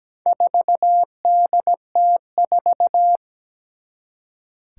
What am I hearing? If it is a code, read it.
Morse code, heard as a keyed tone, about 17 wpm, 700 Hz. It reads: 4DT4
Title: Four Challenging Letter-Number Combinations 17wpm